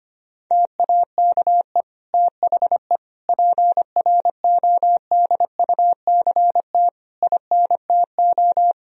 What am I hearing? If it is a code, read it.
Morse code, heard as a keyed tone, about 25 wpm, 700 Hz. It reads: TAKE THE PRODUCT INTO
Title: Sentences from Top 1000 Words - Review Part 2 - 25wpm